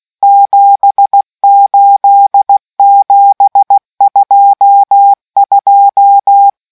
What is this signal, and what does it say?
Morse code, heard as a keyed tone, about 16 wpm, 800 Hz. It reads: 78722